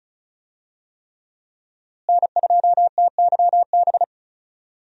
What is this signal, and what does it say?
Morse code, heard as a keyed tone, about 36 wpm, 700 Hz. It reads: N2TYB
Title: USA Call Signs - 5 Characters - Repeated 3x - 35wpm